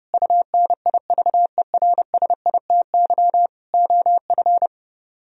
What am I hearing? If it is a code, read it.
Morse code, heard as a keyed tone, about 30 wpm, 700 Hz. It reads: UNIVERSITY OF